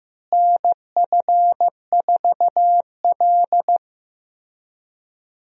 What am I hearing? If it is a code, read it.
Morse code, heard as a keyed tone, about 15 wpm, 700 Hz. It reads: NF4L
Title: USA Call Signs - 4 Characters - Repeated 3x - 15wpm